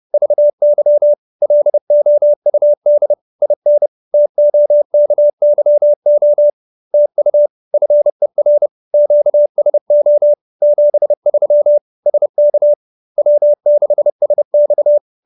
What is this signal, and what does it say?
Morse code, heard as a keyed tone, about 30 wpm, 590 Hz. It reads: VY LOUD IN TOKYO TU FER QSO 73 SK W6SX